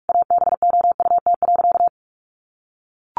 Morse code, heard as a keyed tone, about 44 wpm, 700 Hz. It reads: ABOUT.
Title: Sentences from Top 1400 Words 45wpm